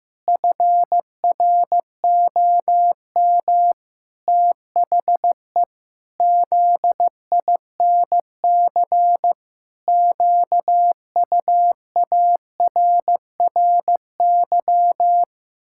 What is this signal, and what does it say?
Morse code, heard as a keyed tone, about 15 wpm, 700 Hz. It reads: FROM THE ZINC QUARRY
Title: Pangram Challenge v2 - 15wpm